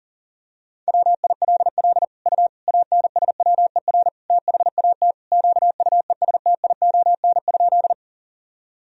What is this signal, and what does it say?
Morse code, heard as a keyed tone, about 37 wpm, 700 Hz. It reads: WILL U ANSWER THAT QUESTION?
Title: Sentences from Top 400 Words and Abbreviations 40wpm